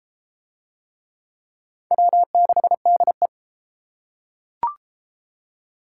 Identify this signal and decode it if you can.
Morse code, heard as a keyed tone, about 33 wpm, 700 Hz. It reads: W6DE